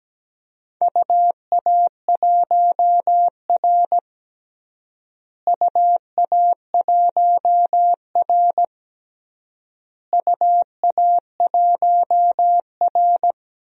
Morse code, heard as a keyed tone, about 17 wpm, 700 Hz. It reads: UA1R UA1R UA1R
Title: DXCC-100 Call Signs - 4 Characters - Repeated 3x - 17wpm